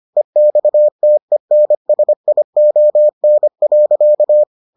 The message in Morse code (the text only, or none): EXTENSION.